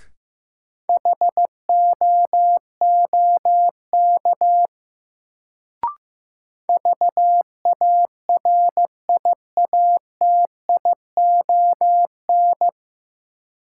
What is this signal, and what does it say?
Morse code, heard as a keyed tone, about 15 wpm, 700 Hz. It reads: HOOK VARIATION
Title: Sentences from Top 1600 Words - New Word Drill Alt - 15wpm